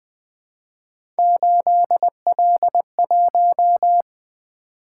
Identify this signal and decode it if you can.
Morse code, heard as a keyed tone, about 20 wpm, 700 Hz. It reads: 8L1
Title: Three Letter-Number Combinations - 20wpm